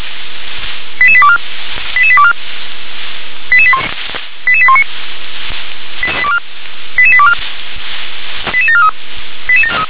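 Five-tone selective call (ZVEI), heard as quick runs of five tones, tones of about 70 ms. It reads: 88824 88824 88819 88824 88743
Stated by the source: ZVEI five-tone call 88832 received by SDR.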